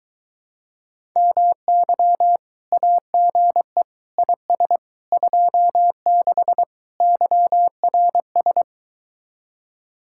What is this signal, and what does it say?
Morse code, heard as a keyed tone, about 22 wpm, 700 Hz. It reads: MY AGE IS 26 YRS